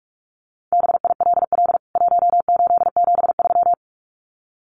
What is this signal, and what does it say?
Morse code, heard as a keyed tone, about 44 wpm, 700 Hz. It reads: BILL 1873
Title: CWT Exchange 45wpm